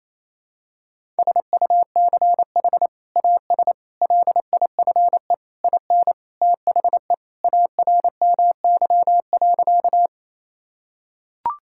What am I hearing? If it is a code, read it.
Morse code, heard as a keyed tone, about 28 wpm, 700 Hz. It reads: SUCH AS LIFE IN THE ARMY.
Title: Sentences from Top 1000 Words - Review Part 5 - 28wpm